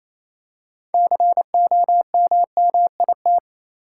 Morse code, heard as a keyed tone, about 28 wpm, 700 Hz. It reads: COMMIT
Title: Six Letter Words - 28wpm